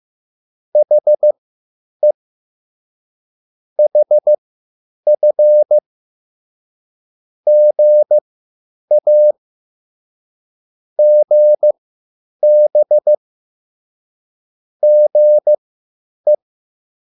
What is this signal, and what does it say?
Morse code, heard as a keyed tone, about 15 wpm, spaced as for 9 wpm, 600 Hz. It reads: HE HF GA GB GE
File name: QSL / CW lesson